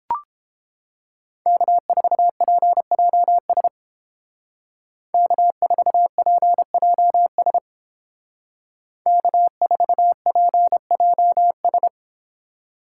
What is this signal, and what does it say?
Morse code, heard as a keyed tone, about 30 wpm, 700 Hz. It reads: K4PJS K4PJS K4PJS